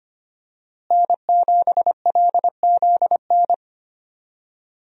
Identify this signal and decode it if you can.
Morse code, heard as a keyed tone, about 25 wpm, 700 Hz. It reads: N7LZN
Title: USA Call Signs - 5 Characters - Repeated 2x - 25wpm